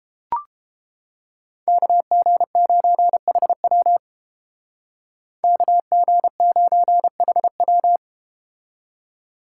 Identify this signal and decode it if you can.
Morse code, heard as a keyed tone, about 32 wpm, 700 Hz. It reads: KG9HW KG9HW